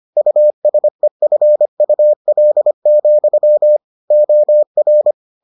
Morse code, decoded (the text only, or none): USEFUL, OR